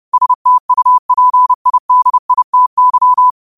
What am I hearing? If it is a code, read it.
Morse code, heard as a keyed tone, about 30 wpm, 1000 Hz. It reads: STUPIDITY
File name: qotdfiles / 30wpm